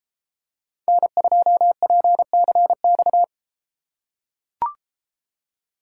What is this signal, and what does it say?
Morse code, heard as a keyed tone, about 32 wpm, 700 Hz. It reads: N2PCX